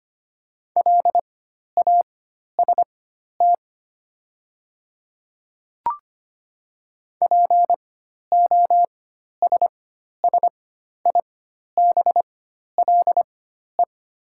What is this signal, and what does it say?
Morse code, heard as a keyed tone, about 25 wpm, spaced as for 12 wpm, 700 Hz. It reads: LAST POSSIBLE
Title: 4x Character Spacing - Top 500 Words - 25wpm